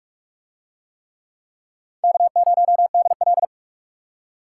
Morse code, heard as a keyed tone, about 44 wpm, 700 Hz. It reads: K0DL